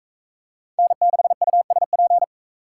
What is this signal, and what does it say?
Morse code, heard as a keyed tone, about 39 wpm, 700 Hz. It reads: N6USP